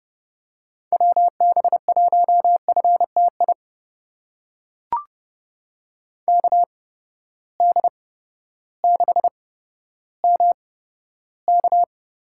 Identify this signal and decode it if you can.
Morse code, heard as a keyed tone, about 32 wpm, spaced as for 14 wpm, 700 Hz. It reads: WB1FTI KD6MK
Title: USA Call Signs - 6 Characters - 8x Character Spacing - 30wpm